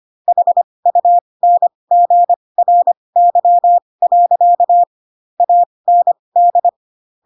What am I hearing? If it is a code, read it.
Morse code, heard as a keyed tone, about 25 wpm, spaced as for 20 wpm, 700 Hz. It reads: HUNGRY. AND
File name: Pinocchio / Pinocchio ch7 15 25 ik1hge